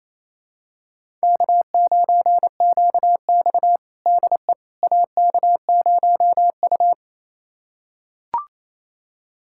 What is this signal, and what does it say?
Morse code, heard as a keyed tone, about 28 wpm, 700 Hz. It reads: K9QX DE AK0U
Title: USA Callsign DE Callsign - 4 Characters - 28wpm